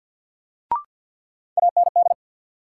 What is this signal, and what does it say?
Morse code, heard as a keyed tone, about 54 wpm, 700 Hz. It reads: AND
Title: Sentences from Top 100 Words - New Word Drill - 50wpm